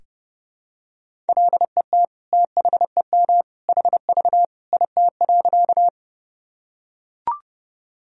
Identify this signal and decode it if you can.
Morse code, heard as a keyed tone, about 30 wpm, 700 Hz. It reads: LET THEM HV IT.
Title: Sentences from Top 300 Words and Abbreviations 30wpm